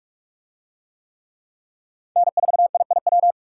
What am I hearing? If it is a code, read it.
Morse code, heard as a keyed tone, about 44 wpm, 700 Hz. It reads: N4IIW